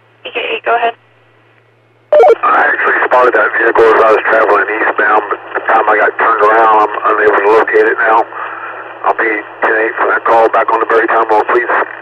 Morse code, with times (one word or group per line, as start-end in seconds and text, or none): none